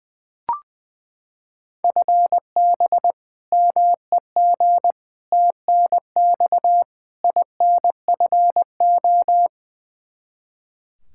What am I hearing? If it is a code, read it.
Morse code, heard as a keyed tone, about 20 wpm, 700 Hz. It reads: FB MEG TNX INFO
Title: Practice QSO - Exchange 3 - More Info - 20wpm